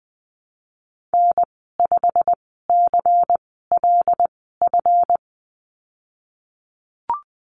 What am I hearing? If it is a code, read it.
Morse code, heard as a keyed tone, about 20 wpm, spaced as for 15 wpm, 700 Hz. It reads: N5CLF
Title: 154 Lesson 25g - Letter p - Call Signs 20wpm